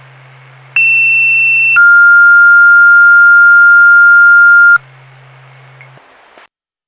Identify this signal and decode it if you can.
Two-tone page, heard as a one-second tone, then a three-second tone, about 2700 then 1400 Hz. It is heard from 0.8 to 4.8 s.